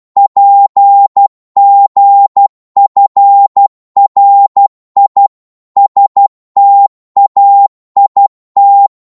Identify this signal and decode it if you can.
Morse code, heard as a keyed tone, about 12 wpm, 800 Hz. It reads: PGFRI STAIT
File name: Morse / ltr-12wpm